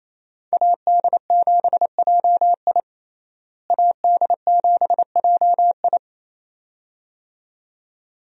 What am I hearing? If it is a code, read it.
Morse code, heard as a keyed tone, about 28 wpm, 700 Hz. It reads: AD7JI AD7JI
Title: USA Call Signs - 5 Characters - Repeated 2x - 28wpm